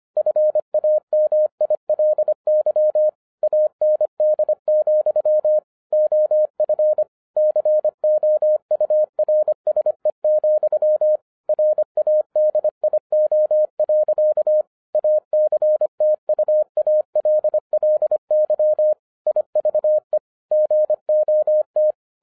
Morse code, heard as a keyed tone, about 25 wpm, 610 Hz. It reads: FAMILY AND, OF COURSE, RADIO. ACTUALLY IVE GOT
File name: 2011-10-mp3 / n04-JUST MOVED 25wpm